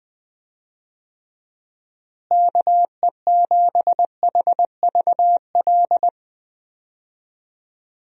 Morse code, heard as a keyed tone, about 20 wpm, 700 Hz. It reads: KE7HVL
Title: USA Call Signs - 6 Characters - Repeated 2x - 20wpm